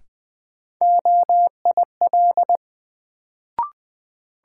Morse code, heard as a keyed tone, about 20 wpm, 700 Hz. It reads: OIL
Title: Sentences from Top 1000 Words - Review Part 4 - New Word Drill - 20wpm